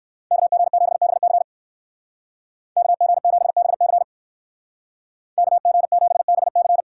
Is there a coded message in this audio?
Yes: Morse code from 0.3 to 6.8 s, heard as a keyed tone, about 58 wpm, 700 Hz. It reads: KG7BC KG7BC KG7BC